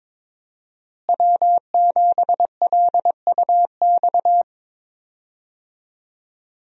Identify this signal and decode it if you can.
Morse code, heard as a keyed tone, about 22 wpm, 700 Hz. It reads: W7LUX
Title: USA Call Signs - 5 Characters - Repeated 3x - 22wpm